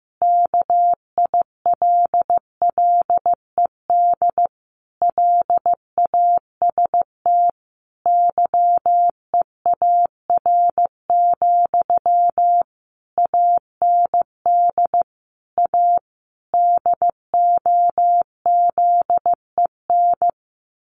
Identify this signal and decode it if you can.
Morse code, heard as a keyed tone, about 15 wpm, 700 Hz. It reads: KILLED LAST YEAR, AND A DOZEN
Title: A Sound Of Thunder 15wpm